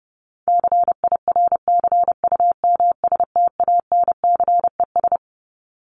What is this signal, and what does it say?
Morse code, heard as a keyed tone, about 30 wpm, 700 Hz. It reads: CIRCUMSTANCES
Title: ES:S Suffix - Top 100 Words with Base Words 30wpm